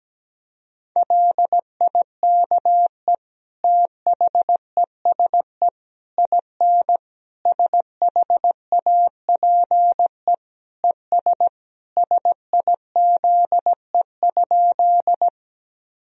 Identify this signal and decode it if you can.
Morse code, heard as a keyed tone, about 17 wpm, 700 Hz. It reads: LIKE THESE IN SHAPE ES SIZE?
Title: Sentences from Top 500 Words and Abbreviations 17wpm